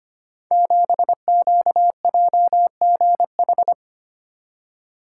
Morse code, heard as a keyed tone, about 25 wpm, 700 Hz. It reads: 7QJGH
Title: Code Groups - 5 Letter:Numbers - 25wpm